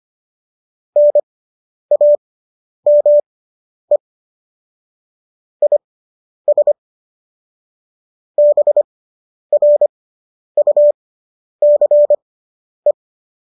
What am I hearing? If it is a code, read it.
Morse code, heard as a keyed tone, about 25 wpm, spaced as for 10 wpm, 590 Hz. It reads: NAME IS BRUCE